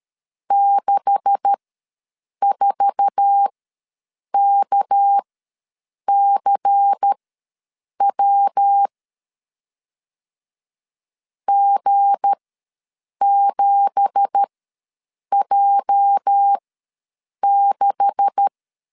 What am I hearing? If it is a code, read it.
Morse code, heard as a keyed tone, about 13 wpm, spaced as for 7 wpm, 800 Hz. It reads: 64KCW G7J6
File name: LEZIONI / 10 SIGNS / 10B INT ws 12wpm 3527